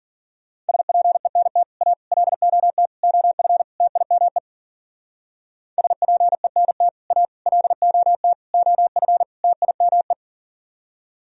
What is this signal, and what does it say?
Morse code, heard as a keyed tone, about 47 wpm, 700 Hz. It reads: SPENT A LOT OF TIME SPENT A LOT OF TIME
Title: Speed-Racing - Sets of 5 Words 40wpm